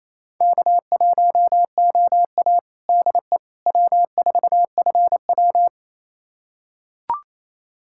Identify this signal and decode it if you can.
Morse code, heard as a keyed tone, about 28 wpm, 700 Hz. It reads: K1OA DE W4FW
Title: USA Callsign DE Callsign - 4 Characters - 28wpm